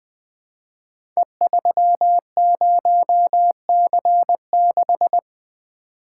Morse code, heard as a keyed tone, about 20 wpm, 700 Hz. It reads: E30C6
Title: Code Groups - 5 Letter:Numbers - 20wpm